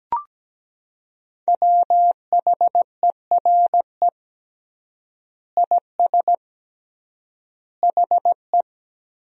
Morse code, heard as a keyed tone, about 17 wpm, 700 Hz. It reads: WHERE IS HE
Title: 3x Word-Spacing - Sentences from Top 200 Words - 17wpm